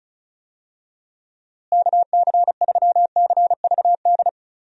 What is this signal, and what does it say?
Morse code, heard as a keyed tone, about 34 wpm, 700 Hz. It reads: KC3CVD
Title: USA Call Signs - 6 Characters - Repeated 3x - 35wpm